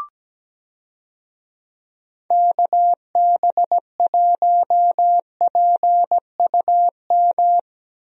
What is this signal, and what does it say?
Morse code, heard as a keyed tone, about 17 wpm, 700 Hz. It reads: KB1PUM
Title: USA Call Signs - 6 Characters - Repeated 2x - 17wpm